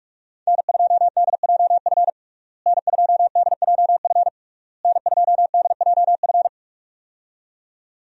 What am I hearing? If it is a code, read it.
Morse code, heard as a keyed tone, about 46 wpm, 700 Hz. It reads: N2DJF N2DJF N2DJF
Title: USA Call Signs - 5 Characters - Repeated 3x - 45wpm